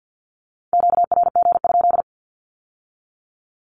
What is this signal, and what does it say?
Morse code, heard as a keyed tone, about 50 wpm, 700 Hz. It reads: QRG?
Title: Q-Codes with Spoken Meaning - 50wpm